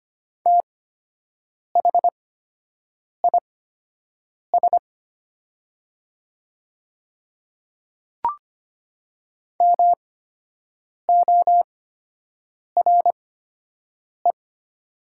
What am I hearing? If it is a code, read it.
Morse code, heard as a keyed tone, about 25 wpm, spaced as for 7 wpm, 700 Hz. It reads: THIS MORE
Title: 8x Character Spacing - Top 100 Words - 25wpm